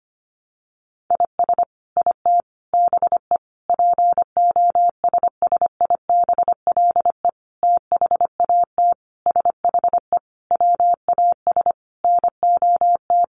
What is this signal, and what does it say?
Morse code, heard as a keyed tone, about 25 wpm, 700 Hz. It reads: IS IT BE POSSIBLE THAT SHE WAS NOT
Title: Sentences from Top 1200 Words 25wpm